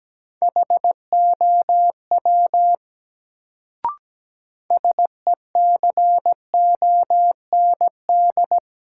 Morse code, heard as a keyed tone, about 17 wpm, 700 Hz. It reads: HOW SECOND
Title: Sets of 2 Words - New Word Drill - 17wpm